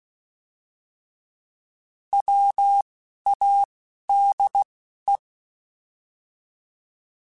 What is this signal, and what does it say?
Morse code, heard as a keyed tone, about 16 wpm, 800 Hz. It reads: W A D E
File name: Alphabetical / Alphabetical mp3s / Ws @10